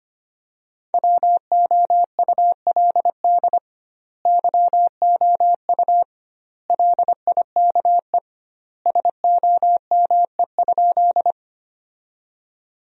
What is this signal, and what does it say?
Morse code, heard as a keyed tone, about 25 wpm, 700 Hz. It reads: WOULD YOU LIKE SOME?
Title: 2x Word-Spacing - Sentences from Top 100 Words - 25wpm